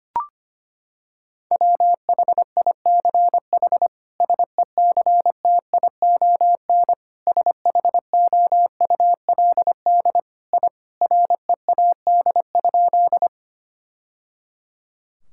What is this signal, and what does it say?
Morse code, heard as a keyed tone, about 25 wpm, 700 Hz. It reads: WHICH SECTION SHOULD I READ?
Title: Sentences from Top 1000 Words - Review Part 5 - 25wpm